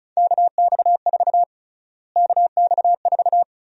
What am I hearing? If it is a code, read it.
Morse code, heard as a keyed tone, about 36 wpm, 700 Hz. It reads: KX4 KX4
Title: USA Call Sign Prefixes - Repeated 2x - 35wpm